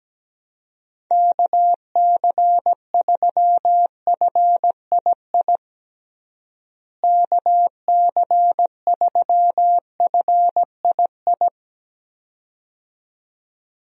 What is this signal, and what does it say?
Morse code, heard as a keyed tone, about 17 wpm, 700 Hz. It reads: KC3FII KC3FII
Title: USA Call Signs - 6 Characters - Repeated 2x - 17wpm